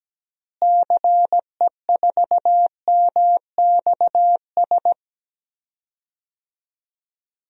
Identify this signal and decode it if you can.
Morse code, heard as a keyed tone, about 17 wpm, 700 Hz. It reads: CE4MXS
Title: DXCC-100 Call Signs - 6 Characters - Repeated 3x - 17wpm